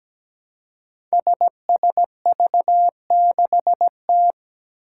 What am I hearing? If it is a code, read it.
Morse code, heard as a keyed tone, about 17 wpm, 700 Hz. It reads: SSV6T
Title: Five Challenging Letter-Number Combinations 17wpm